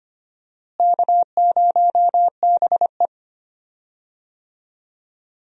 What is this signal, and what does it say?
Morse code, heard as a keyed tone, about 25 wpm, 700 Hz. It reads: K0BE
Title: USA Call Signs - 4 Characters - Repeated 3x - 25wpm